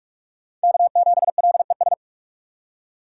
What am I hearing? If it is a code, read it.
Morse code, heard as a keyed tone, about 46 wpm, 700 Hz. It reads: K7LES
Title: USA Call Signs - 5 Characters - 3x Character Spacing - 45wpm